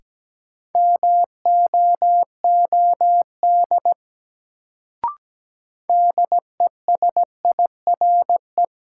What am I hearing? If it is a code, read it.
Morse code, heard as a keyed tone, about 17 wpm, 700 Hz. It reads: MOOD DESIRE
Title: Sentences from Top 1100 Words - New Word Drill - 17wpm